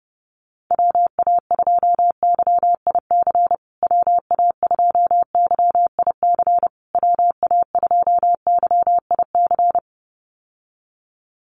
Morse code, heard as a keyed tone, about 30 wpm, 700 Hz. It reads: WA2YIC WA2YIC WA2YIC
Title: USA Call Signs Repeated 3x 30wpm